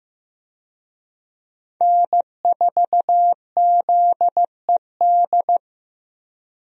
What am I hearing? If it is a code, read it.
Morse code, heard as a keyed tone, about 15 wpm, 700 Hz. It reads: N4ZED
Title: USA Call Signs - 5 Characters - Repeated 3x - 15wpm